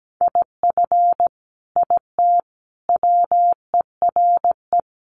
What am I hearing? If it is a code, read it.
Morse code, heard as a keyed tone, about 17 wpm, 700 Hz. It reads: IF IT WERE
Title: Sets of 5 Words 17wpm